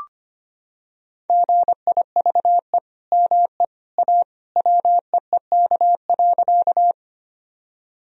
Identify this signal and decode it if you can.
Morse code, heard as a keyed tone, about 25 wpm, 700 Hz. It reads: GIVE ME A WEEK.
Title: Sentences from Top 1000 Words - Review Part 2 - 25wpm